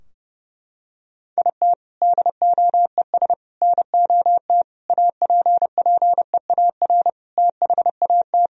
Morse code, heard as a keyed tone, about 32 wpm, 700 Hz. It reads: IT DOES NOT APPEAR THAT
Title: Sentences from Top 1000 Words - Review Part 2 - 30wpm